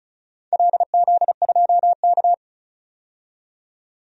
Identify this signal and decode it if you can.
Morse code, heard as a keyed tone, about 36 wpm, 700 Hz. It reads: LZ2K
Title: DXCC-100 Call Signs - 4 Characters - 35wpm